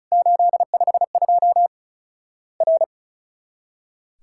Morse code, heard as a keyed tone, about 36 wpm, 700 Hz. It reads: 852 R
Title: Worked All Europe DX Contest - QTC Exchange - 35wpm